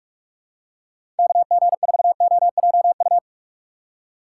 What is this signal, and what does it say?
Morse code, heard as a keyed tone, about 46 wpm, 700 Hz. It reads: KG4OJU
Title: USA Call Signs - 6 Characters - 3x Character Spacing - 45wpm